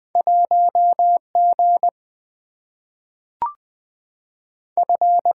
Morse code, heard as a keyed tone, about 20 wpm, 700 Hz. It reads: 1 G F
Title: Two Letter-Number Combinations - 20wpm